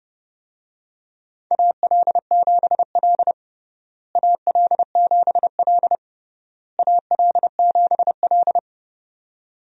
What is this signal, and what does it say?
Morse code, heard as a keyed tone, about 32 wpm, 700 Hz. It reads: AL7L AL7L AL7L
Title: USA Call Signs - 4 Characters - Repeated 3x - 30wpm